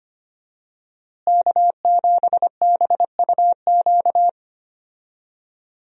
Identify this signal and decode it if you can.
Morse code, heard as a keyed tone, about 25 wpm, 700 Hz. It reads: K7BUQ
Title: USA Call Signs - 5 Characters - Repeated 2x - 25wpm